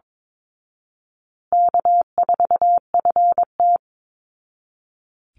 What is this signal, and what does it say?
Morse code, heard as a keyed tone, about 22 wpm, 700 Hz. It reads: K4FT
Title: Call Signs Repeated 3x - 22wpm